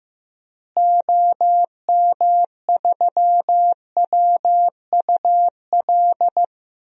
Morse code, heard as a keyed tone, about 15 wpm, 700 Hz. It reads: OM3WUL
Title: DXCC-100 Call Signs - 6 Characters - Repeated 3x - 15wpm